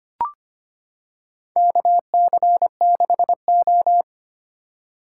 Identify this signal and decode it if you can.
Morse code, heard as a keyed tone, about 25 wpm, 700 Hz. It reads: KC6O
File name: practice / 25wpm / Four Letter-Number Combinations - 25wpm